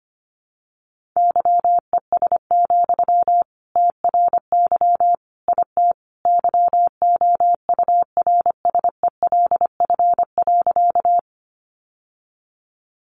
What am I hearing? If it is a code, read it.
Morse code, heard as a keyed tone, about 25 wpm, 700 Hz. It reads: YES, TRY IT YOURSELF.
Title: Sentences from Top 600 Words 25wpm